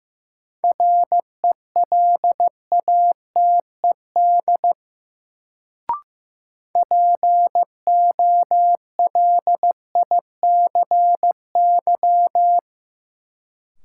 Morse code, heard as a keyed tone, about 15 wpm, 700 Hz. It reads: RELATED POLICY
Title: Sentences from Top 1000 Words - Review Part 5 - New Word Drill - 15wpm